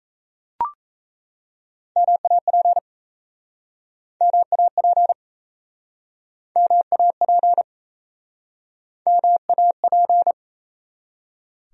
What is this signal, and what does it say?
Morse code, heard as a keyed tone, about 41 wpm, 700 Hz. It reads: MAP MAP MAP MAP